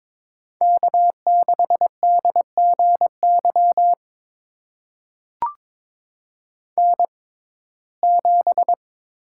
Morse code, heard as a keyed tone, about 22 wpm, spaced as for 16 wpm, 700 Hz. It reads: K6DGY N7
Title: USA Call Signs - 5 Characters - 6x Character Spacing - 22wpm